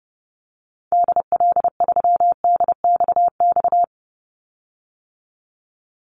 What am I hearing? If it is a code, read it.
Morse code, heard as a keyed tone, about 30 wpm, 700 Hz. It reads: DL3DXX
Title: Call Signs - 30wpm